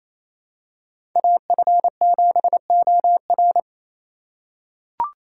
Morse code, heard as a keyed tone, about 28 wpm, 700 Hz. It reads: AF7OR
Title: USA Call Signs - 5 Characters - 2x Character Spacing - 28wpm